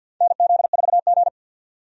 Morse code, heard as a keyed tone, about 51 wpm, 700 Hz. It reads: NZ4G
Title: USA Call Signs - 4 Characters - 50wpm